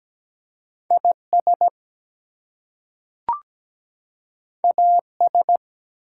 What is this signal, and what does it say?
Morse code, heard as a keyed tone, about 17 wpm, 700 Hz. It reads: IS AS
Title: Two Letter Words - 17wpm